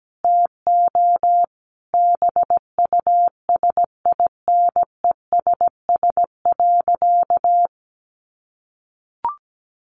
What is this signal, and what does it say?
Morse code, heard as a keyed tone, about 17 wpm, 700 Hz. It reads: TO BUSINESS.
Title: Sentences from Top 800 Words 17wpm